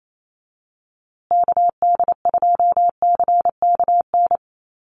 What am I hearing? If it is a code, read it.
Morse code, heard as a keyed tone, about 28 wpm, 700 Hz. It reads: KD2CKN
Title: USA Call Signs Repeated 2x 28wpm